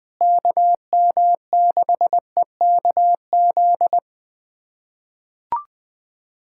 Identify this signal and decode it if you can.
Morse code, heard as a keyed tone, about 20 wpm, 700 Hz. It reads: KM6EKZ
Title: USA Callsign DE Callsign 20wpm